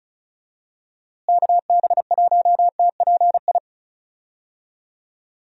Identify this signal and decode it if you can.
Morse code, heard as a keyed tone, about 34 wpm, 700 Hz. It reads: KB1TPI
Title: USA Call Signs - 6 Characters - Repeated 3x - 35wpm